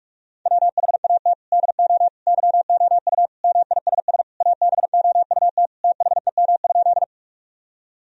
Morse code, heard as a keyed tone, about 46 wpm, 700 Hz. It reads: WHAT DO YOU MISS ABOUT THEM?
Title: Sentences from Top 1000 Words - Review Part 3 - 45wpm